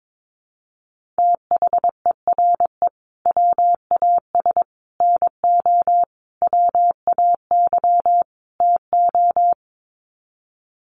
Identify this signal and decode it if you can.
Morse code, heard as a keyed tone, about 22 wpm, 700 Hz. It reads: THERE WAS NO WAY TO
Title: Sets of 5 Words Encore 22wpm